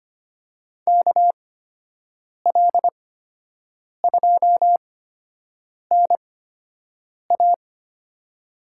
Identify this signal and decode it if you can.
Morse code, heard as a keyed tone, about 25 wpm, spaced as for 7 wpm, 700 Hz. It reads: KL2NA
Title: USA Call Signs - 8x Character Spacing - 25wpm